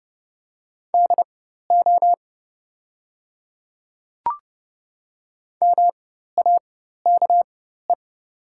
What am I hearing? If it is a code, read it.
Morse code, heard as a keyed tone, about 30 wpm, spaced as for 14 wpm, 700 Hz. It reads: DO MAKE
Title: 4x Character Spacing - Top 100 Words - 30wpm